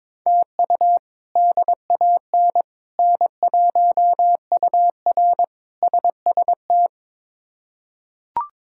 Morse code, heard as a keyed tone, about 22 wpm, 700 Hz. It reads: TU DAN N1UR SST
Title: K1USN SST QSO Practice - 22wpm